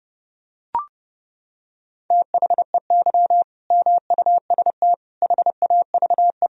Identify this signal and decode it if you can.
Morse code, heard as a keyed tone, about 32 wpm, 700 Hz. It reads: THEY MUST HAVE
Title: Sentences from Top 1000 Words - Review Part 4 - 30wpm